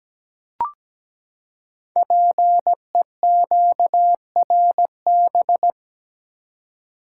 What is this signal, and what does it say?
Morse code, heard as a keyed tone, about 17 wpm, 700 Hz. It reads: PEQRB
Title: Code Groups - 5 Letters - 17wpm